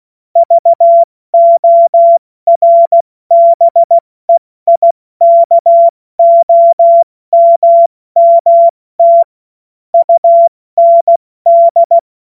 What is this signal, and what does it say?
Morse code, heard as a keyed tone, about 16 wpm, spaced as for 14 wpm, 680 Hz. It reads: VORBEIKOMMT UND
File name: Morsekurs / Witze / Witz 31. Mai 2022 70 F80